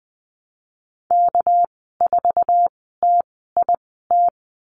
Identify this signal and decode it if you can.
Morse code, heard as a keyed tone, about 20 wpm, spaced as for 15 wpm, 700 Hz. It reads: K4TIT
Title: 207 Lesson 33g - Letter j - Call Signs 20wpm